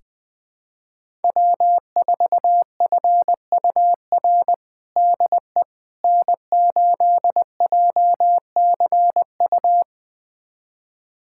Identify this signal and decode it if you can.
Morse code, heard as a keyed tone, about 20 wpm, 700 Hz. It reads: W4FUR DE N8JCU
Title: USA Callsign DE Callsign - 5 Characters - 20wpm